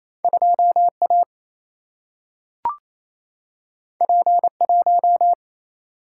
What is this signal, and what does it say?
Morse code, heard as a keyed tone, about 28 wpm, 700 Hz. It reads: 2 A P 1
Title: Single-Letter Single-Number 28wpm